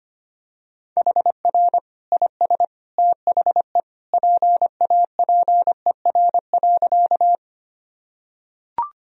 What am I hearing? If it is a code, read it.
Morse code, heard as a keyed tone, about 25 wpm, 700 Hz. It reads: HR IS THE PAPER.
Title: Sentences from Top 300 Words and Abbreviations 25wpm